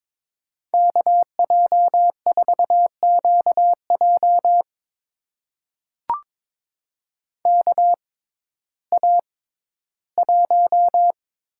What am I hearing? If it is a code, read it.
Morse code, heard as a keyed tone, about 22 wpm, spaced as for 13 wpm, 700 Hz. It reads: KJ4QJ KA1
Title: USA Call Signs - 6x Character Spacing - 22wpm